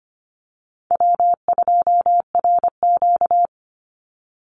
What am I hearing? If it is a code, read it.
Morse code, heard as a keyed tone, about 24 wpm, 700 Hz. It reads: W2RQ
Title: Call Signs Repeated 2x - 25wpm